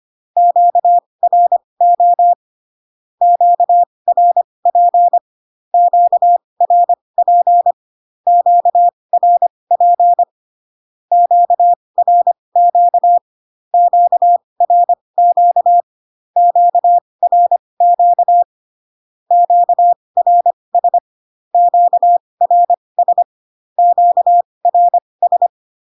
Morse code, heard as a keyed tone, about 25 wpm, spaced as for 20 wpm, 700 Hz. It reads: QRO QRP QRP QRP QRQ QRQ QRQ QRS QRS QRS
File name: cw / Q-codes most common repeated 3 times 15 25wpm ik1hge